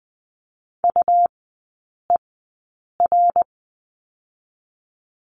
Morse code, heard as a keyed tone, about 20 wpm, spaced as for 8 wpm, 700 Hz. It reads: UER
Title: 218 Lesson 35e - Number 0 - Three Characters 20wpm